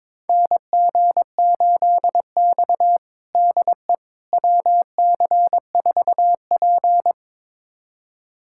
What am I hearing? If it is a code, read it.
Morse code, heard as a keyed tone, about 22 wpm, 700 Hz. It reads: NG8X DE WC4P
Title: USA Callsign DE Callsign - 4 Characters - 22wpm